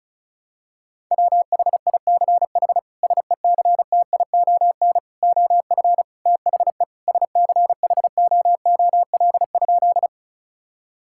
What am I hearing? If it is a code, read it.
Morse code, heard as a keyed tone, about 36 wpm, 700 Hz. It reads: WHICH SECTION OF THE SCHOOL?
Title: Sentences from Top 1000 Words - Review Part 5 - 35wpm